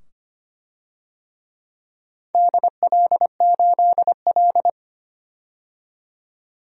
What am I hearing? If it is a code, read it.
Morse code, heard as a keyed tone, about 25 wpm, 700 Hz. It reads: DL8L
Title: DXCC-100 Call Signs - 4 Characters - 25wpm